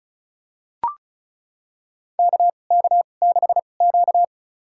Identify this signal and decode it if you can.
Morse code, heard as a keyed tone, about 36 wpm, spaced as for 25 wpm, 700 Hz. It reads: KK6Q
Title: USA Call Signs - 2x Character Spacing - 35wpm